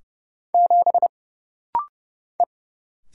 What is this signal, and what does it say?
Morse code, heard as a keyed tone, about 29 wpm, 700 Hz. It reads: 7 E
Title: Single Letter-Number - Rapid-Fire 30wpm